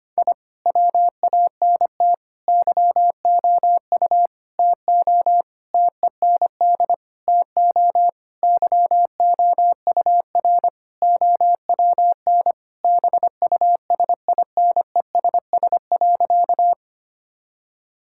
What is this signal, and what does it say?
Morse code, heard as a keyed tone, about 25 wpm, 700 Hz. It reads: I WANT YOU TO TEND TO YOUR OWN BUSINESS.
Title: Sentences from Top 1000 Words - Review Part 5 - 25wpm